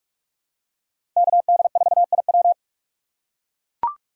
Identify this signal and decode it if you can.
Morse code, heard as a keyed tone, about 41 wpm, 700 Hz. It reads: KD4IW
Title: USA Call Signs - 5 Characters - 2x Character Spacing - 45wpm